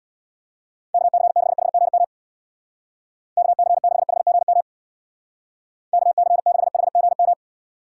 Morse code, heard as a keyed tone, about 60 wpm, 700 Hz. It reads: KC6HGM KC6HGM KC6HGM
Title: Speed Racing - USA 2x3 Call Signs 50wpm